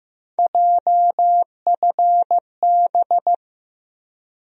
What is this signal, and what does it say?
Morse code, heard as a keyed tone, about 15 wpm, 700 Hz. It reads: JFB
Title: USA Call Sign Postfixes - 3 Letters 15wpm